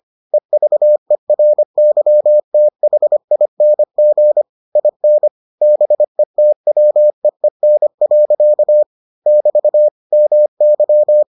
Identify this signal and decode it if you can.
Morse code, heard as a keyed tone, about 25 wpm, 600 Hz. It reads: EVERYTHING IN BETWEEN. <BT> MY